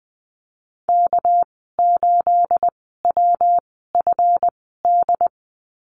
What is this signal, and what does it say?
Morse code, heard as a keyed tone, about 20 wpm, spaced as for 14 wpm, 700 Hz. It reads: K8WFD
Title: 213 Lesson 34g - Number 8 - Call Signs 20wpm